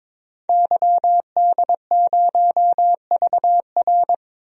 Morse code, heard as a keyed tone, about 21 wpm, 700 Hz. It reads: YD0VR
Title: DXCC-100 Call Signs - 5 Characters - 22wpm